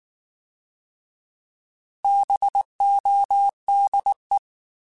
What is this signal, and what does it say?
Morse code, heard as a keyed tone, about 19 wpm, 800 Hz. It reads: BODE
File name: Alphabetical / Alphabetical mp3s / Bs @18